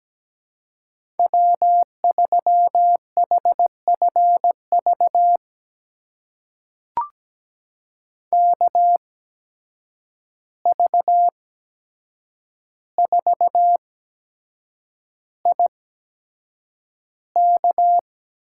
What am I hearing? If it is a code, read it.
Morse code, heard as a keyed tone, about 17 wpm, spaced as for 7 wpm, 700 Hz. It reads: W3HFV KV4IK
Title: USA Call Signs - 5 Characters - 8x Character Spacing - 17wpm